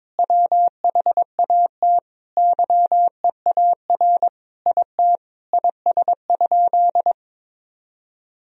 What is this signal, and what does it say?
Morse code, heard as a keyed tone, about 21 wpm, 700 Hz. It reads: WHAT YEAR IT IS?
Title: Sentences from Top 1000 Words - Review Part 1 - 22wpm